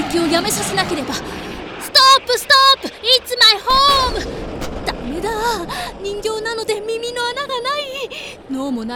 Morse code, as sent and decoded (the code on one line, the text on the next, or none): none